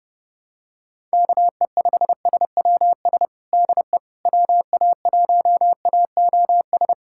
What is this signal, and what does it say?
Morse code, heard as a keyed tone, about 30 wpm, 690 Hz. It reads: KE5SWS DE WA1AOS
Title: USA Callsign DE Callsign - 6 Characters - 30wpm